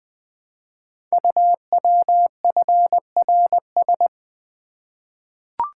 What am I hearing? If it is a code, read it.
Morse code, heard as a keyed tone, about 20 wpm, 700 Hz. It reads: UWFRS
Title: Code Groups - 5 Letters - 20wpm